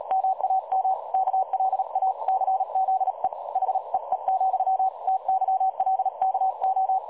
Morse code, heard as a keyed tone, about 37 wpm, 760 Hz. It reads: MAM KLACZE HEE, TYLKO
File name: dlazaawansowanych / nagraniazpasma / websdr recording 3565.0kHz